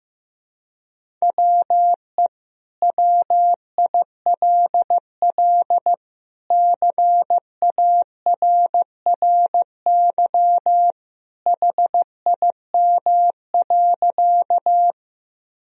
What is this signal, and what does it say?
Morse code, heard as a keyed tone, about 16 wpm, 700 Hz. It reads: WE WILL CARRY HIM.
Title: Sentences from Top 1000 Words - Review Part 1 - 15wpm